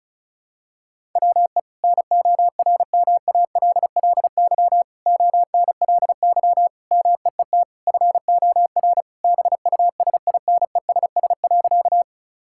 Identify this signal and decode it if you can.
Morse code, heard as a keyed tone, about 36 wpm, 700 Hz. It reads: WE NORMALLY ONLY MEET FOR BUSINESS.
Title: Sentences from Top 1000 Words - Review Part 6 - 35wpm